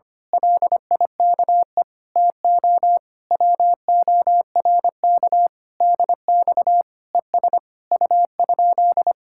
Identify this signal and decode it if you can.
Morse code, heard as a keyed tone, about 24 wpm, 700 Hz. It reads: LIKE TO WORK DX ES U?
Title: Speed-Racing Common QSO Phrases - 25wpm